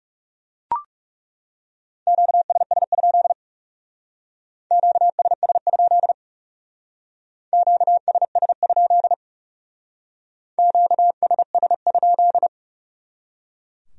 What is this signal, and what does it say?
Morse code, heard as a keyed tone, about 38 wpm, 700 Hz. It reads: QSS? QSS? QSS? QSS?